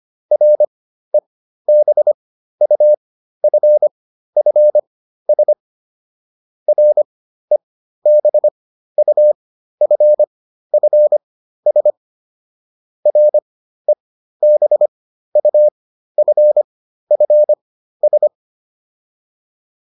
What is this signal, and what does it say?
Morse code, heard as a keyed tone, about 25 wpm, spaced as for 13 wpm, 600 Hz. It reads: REBUFFS REBUFFS REBUFFS